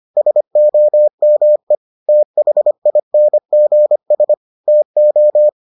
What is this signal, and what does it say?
Morse code, heard as a keyed tone, about 25 wpm, 600 Hz. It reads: SOME THINGS TO